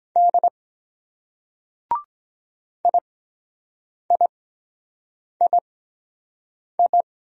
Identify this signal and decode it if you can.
Morse code, heard as a keyed tone, about 22 wpm, spaced as for 7 wpm, 700 Hz. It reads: D IIII